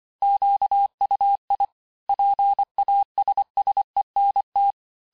Morse code, heard as a keyed tone, about 25 wpm, 800 Hz. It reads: QUI PASSENT
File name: cours-868-teste-clair-vitesse-23mpm